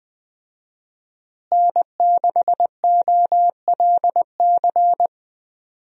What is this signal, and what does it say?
Morse code, heard as a keyed tone, about 21 wpm, 700 Hz. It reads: N6OLC
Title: USA Call Signs - 5 Characters - Repeated 2x - 20wpm